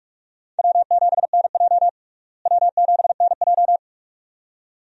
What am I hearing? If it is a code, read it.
Morse code, heard as a keyed tone, about 41 wpm, 700 Hz. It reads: W7NJ W7NJ
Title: USA Call Signs - 4 Characters - Repeated 2x - 45wpm